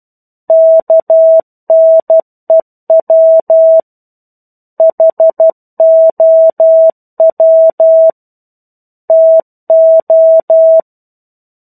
Morse code, heard as a keyed tone, about 12 wpm, 650 Hz. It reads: KNEW HOW TO
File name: 01-Chapter-01 12 WPM